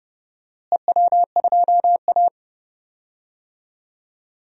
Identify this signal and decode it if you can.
Morse code, heard as a keyed tone, about 30 wpm, 700 Hz. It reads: EW2A
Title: DXCC-100 Call Signs - 4 Characters - Repeated 2x - 30wpm